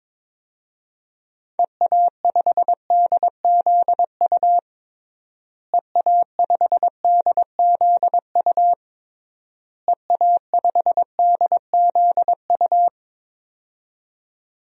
Morse code, heard as a keyed tone, about 22 wpm, 700 Hz. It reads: EA5DZU EA5DZU EA5DZU
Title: DXCC-100 Call Signs - 6 Characters - Repeated 3x - 22wpm